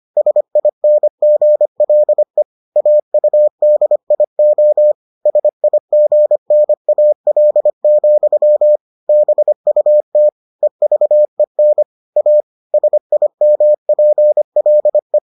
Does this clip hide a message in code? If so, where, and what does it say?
Yes: Morse code from 0.2 to 15.2 s, heard as a keyed tone, about 25 wpm, 600 Hz. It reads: SINGLE AUDIO SIGNAL, BUT EVEN A SIMPLE